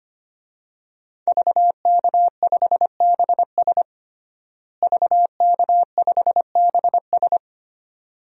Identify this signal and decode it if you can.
Morse code, heard as a keyed tone, about 25 wpm, 700 Hz. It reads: VK5BS VK5BS
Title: DXCC-100 Call Signs - 5 Characters - Repeated 2x - 25wpm